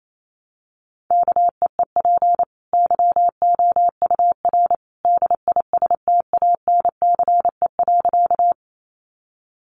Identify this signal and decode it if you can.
Morse code, heard as a keyed tone, about 28 wpm, 700 Hz. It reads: KEEP YOUR DISTANCE.
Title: Sentences from Top 1200 Words 28wpm